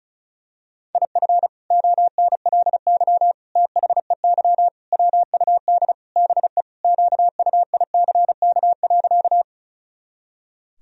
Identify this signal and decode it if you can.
Morse code, heard as a keyed tone, about 36 wpm, 700 Hz. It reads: IF ONLY THEY WUD BE QUICK.
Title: Sentences from Top 500 Words and Abbreviations 35wpm